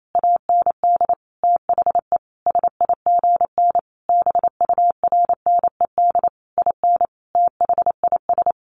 Morse code, heard as a keyed tone, about 28 wpm, 700 Hz. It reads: AND THE SIGN BURNED IN THIS
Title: A Sound Of Thunder 28wpm